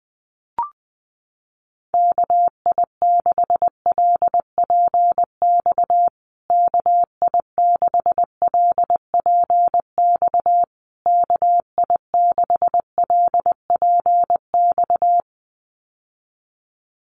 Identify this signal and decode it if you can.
Morse code, heard as a keyed tone, about 20 wpm, 700 Hz. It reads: KI6LPX KI6LPX KI6LPX
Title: USA Call Signs Repeated 3x 20wpm